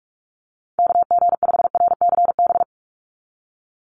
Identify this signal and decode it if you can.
Morse code, heard as a keyed tone, about 46 wpm, 700 Hz. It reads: KG5RCB
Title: USA Call Signs 45wpm